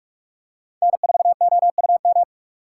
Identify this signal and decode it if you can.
Morse code, heard as a keyed tone, about 43 wpm, 700 Hz. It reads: N 4 O U M
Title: USA Call Signs - 5 Characters - 6x Character Spacing - 45wpm